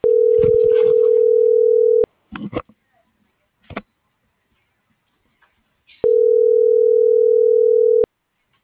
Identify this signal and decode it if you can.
Ringback tone (North America), 440 + 480 Hz.
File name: CENTRAL GOMSA / out aylin.bazan